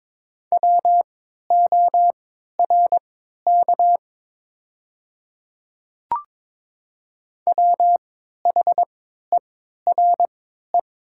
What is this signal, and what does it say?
Morse code, heard as a keyed tone, about 22 wpm, spaced as for 13 wpm, 700 Hz. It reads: WORK WHERE
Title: 3x Character Spacing - Top 300 Words - 22wpm